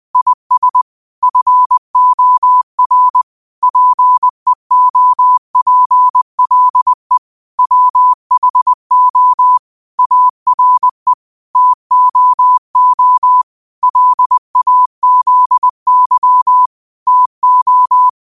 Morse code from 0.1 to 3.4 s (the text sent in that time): IS FOR